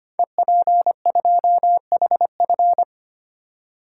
Morse code, heard as a keyed tone, about 25 wpm, 700 Hz. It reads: EP2HF
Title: Code Groups - 5 Letter:Numbers - 25wpm